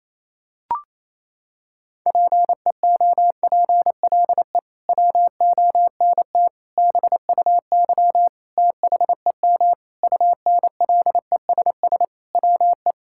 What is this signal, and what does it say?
Morse code, heard as a keyed tone, about 27 wpm, 700 Hz. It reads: PEOPLE WONT BUY THEM UNLESS WE
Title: Sentences from Top 1000 Words - Review Part 6 - 28wpm